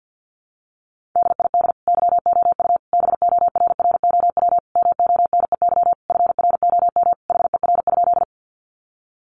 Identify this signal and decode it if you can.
Morse code, heard as a keyed tone, about 50 wpm, 700 Hz. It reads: DID YOU BORROW MONEY FROM HER?
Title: Sentences from Top 1500 Words 50wpm